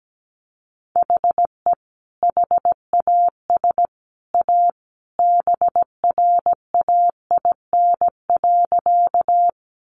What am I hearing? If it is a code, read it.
Morse code, heard as a keyed tone, about 17 wpm, 700 Hz. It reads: HE HAS A BRAIN.
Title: Sentences from Top 1100 Words 17wpm